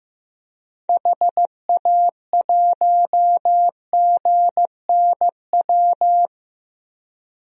Morse code, heard as a keyed tone, about 15 wpm, 700 Hz. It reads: HA1GNW
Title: DXCC-100 Call Signs - 6 Characters - 15wpm